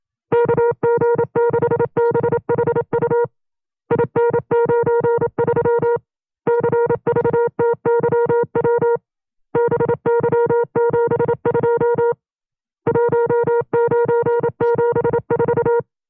Morse code, heard as a keyed tone, about 27 wpm, 470 Hz. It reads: KG6BHU IN93 CVTYW BY72 1974
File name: 25WPM